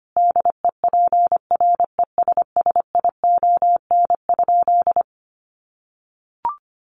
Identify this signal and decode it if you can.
Morse code, heard as a keyed tone, about 25 wpm, 700 Hz. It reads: DEPRESSION?
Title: Sentences from Top 1300 Words 25wpm